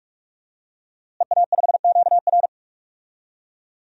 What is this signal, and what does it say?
Morse code, heard as a keyed tone, about 46 wpm, 700 Hz. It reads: EA5QR
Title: DXCC-100 Call Signs - 5 Characters - Repeated 2x - 45wpm